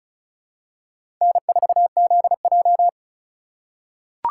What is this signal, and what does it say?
Morse code, heard as a keyed tone, about 35 wpm, 700 Hz. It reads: N4ZJ
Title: USA Call Signs - 4 Characters - 2x Character Spacing - 35wpm